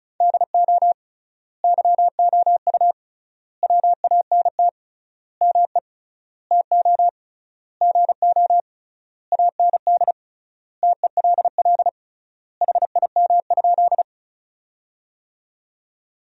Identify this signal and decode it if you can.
Morse code, heard as a keyed tone, about 36 wpm, 700 Hz. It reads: DO YOU WANT ME TO GO AND TELL HIM?
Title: 3x Word-Spacing - Sentences from Top 200 Words - 35wpm